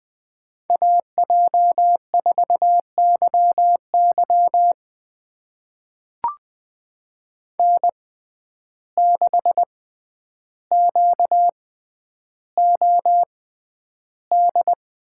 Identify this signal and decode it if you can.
Morse code, heard as a keyed tone, about 20 wpm, spaced as for 10 wpm, 700 Hz. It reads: AJ4YY N6QOD
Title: USA Call Signs - 5 Characters - 6x Character Spacing - 20wpm